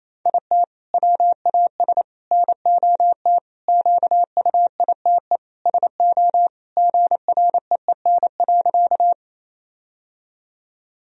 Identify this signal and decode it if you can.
Morse code, heard as a keyed tone, about 28 wpm, 700 Hz. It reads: IT WAS NOT QUITE SO GREEN.
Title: Sentences from Top 1000 Words - Review Part 2 - 28wpm